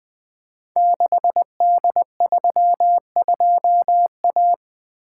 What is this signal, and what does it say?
Morse code, heard as a keyed tone, about 20 wpm, 700 Hz. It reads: 6D32A
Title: Code Groups - 5 Letter:Numbers - 20wpm